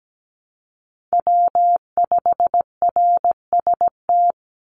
Morse code, heard as a keyed tone, about 17 wpm, 700 Hz. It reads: W5RST
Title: NA SOTA Chaser Callsigns 17wpm